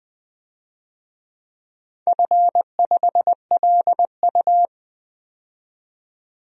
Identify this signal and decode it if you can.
Morse code, heard as a keyed tone, about 20 wpm, 700 Hz. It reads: F5LU
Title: DXCC-100 Call Signs - 4 Characters - 20wpm